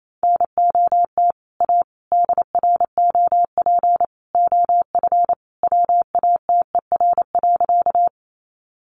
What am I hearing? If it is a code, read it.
Morse code, heard as a keyed tone, about 28 wpm, 700 Hz. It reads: NOT A DROP OF WATER.
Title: Sentences from Top 1400 Words 28wpm